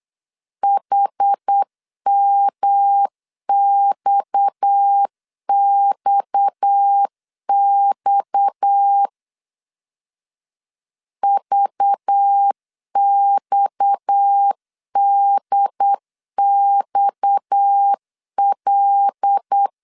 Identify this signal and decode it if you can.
Morse code, heard as a keyed tone, about 8 wpm, 800 Hz. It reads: HMXXX VXDXL
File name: LEZIONI / 6 PKX0 / 6 PKX 8wpm 3041